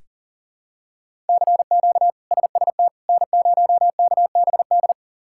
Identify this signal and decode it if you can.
Morse code, heard as a keyed tone, about 40 wpm, 700 Hz. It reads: CQ SST N0KBD
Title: K1USN SST QSO Practice - 40wpm